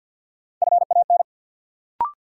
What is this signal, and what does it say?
Morse code, heard as a keyed tone, about 53 wpm, 700 Hz. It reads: FAN
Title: Sentences from Top 1300 Words - New Word Drill Alt - 50wpm